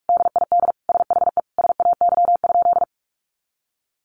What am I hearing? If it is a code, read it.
Morse code, heard as a keyed tone, about 44 wpm, 700 Hz. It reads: DID SHE SAY?
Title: Sentences from Top 200 Words 45wpm